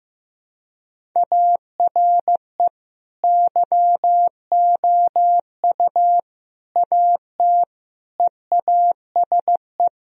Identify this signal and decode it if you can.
Morse code, heard as a keyed tone, about 16 wpm, 700 Hz. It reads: ARE YOU AT EASE
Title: Sentences from Top 1000 Words - Review Part 1 - 15wpm